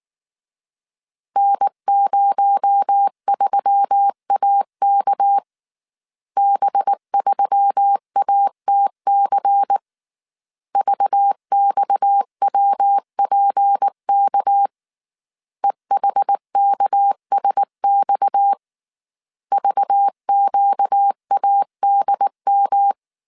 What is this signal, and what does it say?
Morse code, heard as a keyed tone, about 19 wpm, 800 Hz. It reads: N03AK B3ATC VXWPK EHKSX VQADM